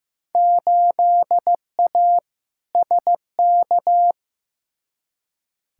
Morse code, heard as a keyed tone, about 15 wpm, 700 Hz. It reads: 8A SK
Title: Field Day Exchanges Encore - 15wpm